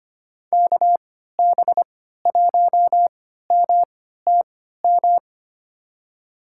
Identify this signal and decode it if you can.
Morse code, heard as a keyed tone, about 25 wpm, spaced as for 14 wpm, 700 Hz. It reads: KB1MTM
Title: USA Call Signs - 6 Characters - 3x Character Spacing - 25wpm